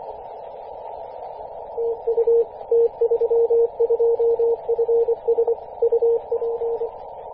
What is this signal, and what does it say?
Morse code, heard as a keyed tone, about 24 wpm, 460 Hz, with radio noise. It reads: TU T32FS UP
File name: t32fs-10mhz-cw